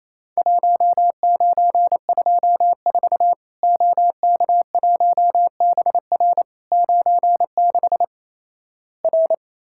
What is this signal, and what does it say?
Morse code, heard as a keyed tone, about 28 wpm, 700 Hz. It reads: 1924 OK1BR 96 R
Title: Worked All Europe DX Contest - QTC Exchange - 28wpm